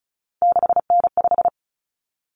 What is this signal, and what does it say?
Morse code, heard as a keyed tone, about 34 wpm, 700 Hz. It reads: 6N5
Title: Common Cut Numbers - 3-Digit 35wpm